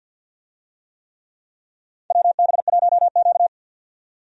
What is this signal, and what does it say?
Morse code, heard as a keyed tone, about 50 wpm, 700 Hz. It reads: WB1Q
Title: USA Call Signs - 4 Characters - 50wpm